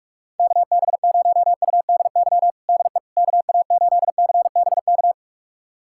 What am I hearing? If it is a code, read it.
Morse code, heard as a keyed tone, about 46 wpm, 700 Hz. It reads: KB0UDY DE KA8CBK
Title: USA Callsign DE Callsign - 6 Characters - 45wpm